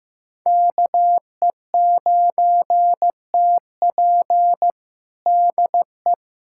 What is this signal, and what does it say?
Morse code, heard as a keyed tone, about 15 wpm, 700 Hz. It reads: KE9TP DE
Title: USA Callsign DE Callsign - 5 Characters - 15wpm